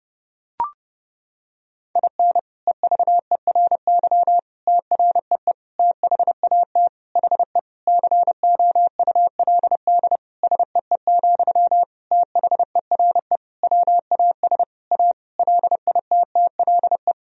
Morse code, heard as a keyed tone, about 30 wpm, 700 Hz. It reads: IN EVERY TREE THAT HE COULD SEE, THERE WAS A LITTLE
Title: Sentences from Top 1000 Words - Review Part 1 - 30wpm